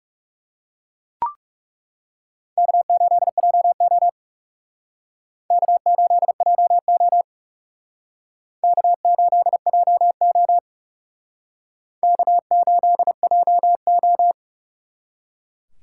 Morse code, heard as a keyed tone, about 37 wpm, 700 Hz. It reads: K8JO K8JO K8JO K8JO
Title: Speed Racing - USA 1x2 Call Signs 30wpm